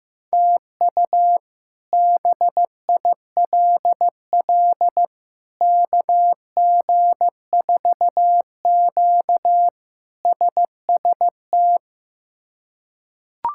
Morse code, heard as a keyed tone, about 15 wpm, 700 Hz. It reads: TU BILL KG4Q SST
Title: K1USN SST QSO Practice - 15wpm